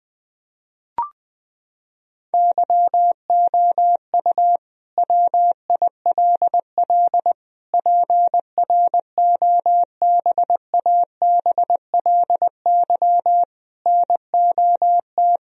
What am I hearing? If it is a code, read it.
Morse code, heard as a keyed tone, about 20 wpm, 700 Hz. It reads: YOU WILL PROBABLY NOT
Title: Sentences from Top 1000 Words - Review Part 4 - 20wpm